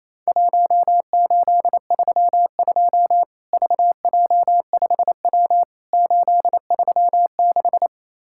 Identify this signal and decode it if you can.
Morse code, heard as a keyed tone, about 27 wpm, 700 Hz. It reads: 1832 VJ5W 836
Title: Worked All Europe DX Contest - QTC Exchange - 28wpm